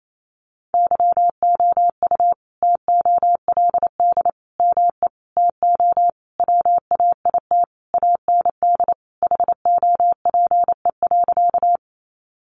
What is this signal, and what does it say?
Morse code, heard as a keyed tone, about 28 wpm, 700 Hz. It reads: YOU TOLD ME TO WAIT AND HOPE.
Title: Sentences from Top 1000 Words 28wpm